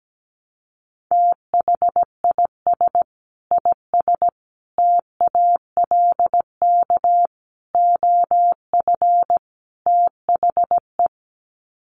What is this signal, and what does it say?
Morse code, heard as a keyed tone, about 17 wpm, 700 Hz. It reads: THIS IS TALK OF THE
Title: Sets of 5 Words 17wpm